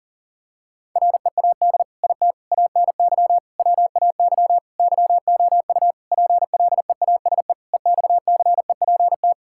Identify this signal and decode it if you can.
Morse code, heard as a keyed tone, about 37 wpm, 700 Hz. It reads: READ IT ANY WAY YOU PLEASE EXCEPT
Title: Sentences from Top 1600 Words 40wpm